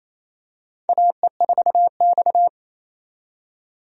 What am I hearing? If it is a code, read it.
Morse code, heard as a keyed tone, about 28 wpm, 700 Hz. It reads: AE4X
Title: USA Call Signs - 4 Characters - 28wpm